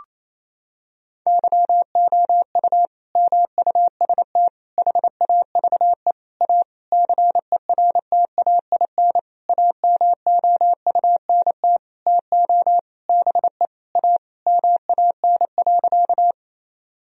Morse code, heard as a keyed tone, about 28 wpm, 700 Hz. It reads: YOU MUST HAVE A CERTAIN AMOUNT TO BE A MAN.
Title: Sentences from Top 1000 Words - Review Part 4 - 28wpm